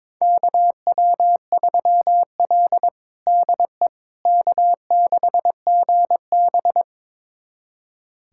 Morse code, heard as a keyed tone, about 22 wpm, 700 Hz. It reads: KW3L DE K6GB
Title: USA Callsign DE Callsign - 4 Characters - 22wpm